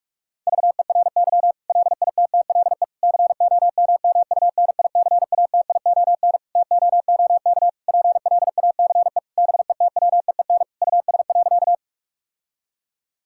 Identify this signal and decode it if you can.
Morse code, heard as a keyed tone, about 46 wpm, 700 Hz. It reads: VERY LITTLE COMMUNICATION TOOK PLACE BETWEEN US.
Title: Sentences from Top 1000 Words - Review Part 6 - 45wpm